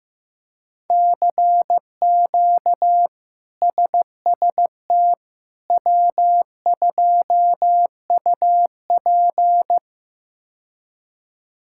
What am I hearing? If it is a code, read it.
Morse code, heard as a keyed tone, about 15 wpm, 700 Hz. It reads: CQ SST W2UP
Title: K1USN SST QSO Practice - 15wpm